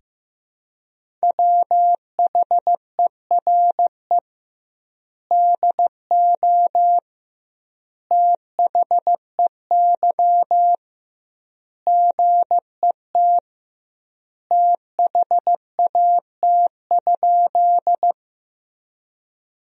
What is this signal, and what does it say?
Morse code, heard as a keyed tone, about 15 wpm, 700 Hz. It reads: WHERE DO THEY GET THAT?
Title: 2x Word-Spacing - Sentences from Top 200 Words - 15wpm